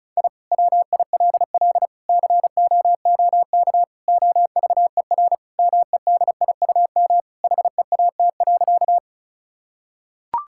Morse code, heard as a keyed tone, about 34 wpm, 700 Hz. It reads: I WILL COOK OVER MEDIUM HEAT.
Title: Sentences from Top 1000 Words - Review Part 6 - 35wpm